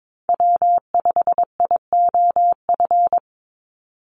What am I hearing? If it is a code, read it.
Morse code, heard as a keyed tone, about 22 wpm, 700 Hz. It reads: W5IOF
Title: USA Call Signs 22wpm